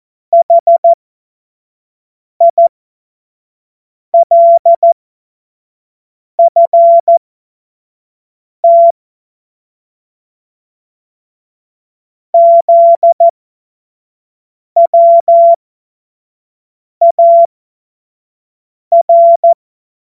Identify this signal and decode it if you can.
Morse code, heard as a keyed tone, about 14 wpm, spaced as for 5 wpm, 680 Hz. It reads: HILFT ZWAR
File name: Morsekurs / Witze / Witz 03. August 2021, 25 F70